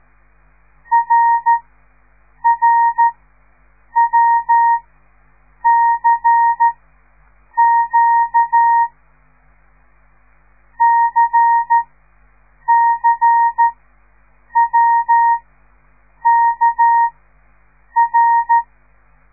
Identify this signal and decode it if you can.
Morse code, heard as a keyed tone, about 13 wpm, spaced as for 7 wpm, 930 Hz. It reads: RRWCQ CCWKR
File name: radiomorningstar / morse / les 5